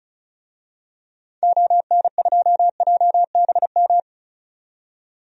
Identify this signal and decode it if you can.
Morse code, heard as a keyed tone, about 36 wpm, 700 Hz. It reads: ON2JBM
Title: DXCC-100 Call Signs - 6 Characters - 35wpm